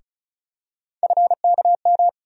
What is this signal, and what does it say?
Morse code, heard as a keyed tone, about 36 wpm, 700 Hz. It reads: FKM
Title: USA Call Sign Postfixes - 3 Letters 35wpm